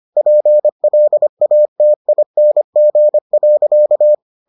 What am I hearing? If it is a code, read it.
Morse code, heard as a keyed tone, about 25 wpm, 600 Hz. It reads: PLATING.